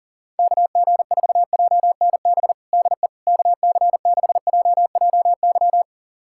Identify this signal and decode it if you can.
Morse code, heard as a keyed tone, about 37 wpm, 700 Hz. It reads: KG4JNB DE KC6JJY
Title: USA Callsign DE Callsign 40wpm